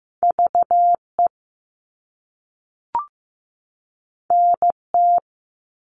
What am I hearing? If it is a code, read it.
Morse code, heard as a keyed tone, about 15 wpm, 700 Hz. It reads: VE NT
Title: All Cut Numbers - 2-Digit 15wpm